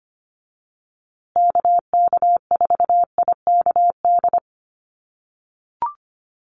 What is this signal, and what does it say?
Morse code, heard as a keyed tone, about 25 wpm, 700 Hz. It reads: KK4IKD
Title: USA Call Signs Repeated 2x 25wpm